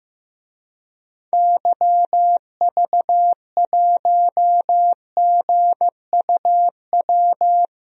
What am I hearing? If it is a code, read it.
Morse code, heard as a keyed tone, about 15 wpm, 700 Hz. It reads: YV1GUW
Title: DXCC-100 Call Signs - 6 Characters - Repeated 3x - 15wpm